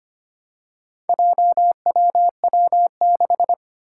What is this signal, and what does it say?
Morse code, heard as a keyed tone, about 25 wpm, 700 Hz. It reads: JWW6
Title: Four Challenging Letter-Number Combinations 25wpm